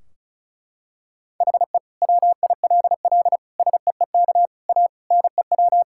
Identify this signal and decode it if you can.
Morse code, heard as a keyed tone, about 34 wpm, 700 Hz. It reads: HE WILL SEEK A NEW
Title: Sentences from Top 1600 Words 35wpm